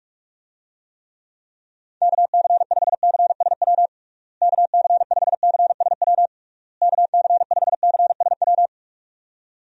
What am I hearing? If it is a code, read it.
Morse code, heard as a keyed tone, about 46 wpm, 700 Hz. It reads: KC5CSW KC5CSW KC5CSW
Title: USA Call Signs - 6 Characters - Repeated 3x - 45wpm